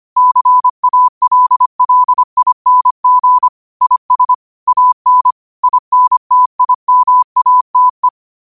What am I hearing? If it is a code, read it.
Morse code, heard as a keyed tone, about 25 wpm, 1000 Hz. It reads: CALLING IS AN INTIMATE
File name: qotdfiles / 25wpm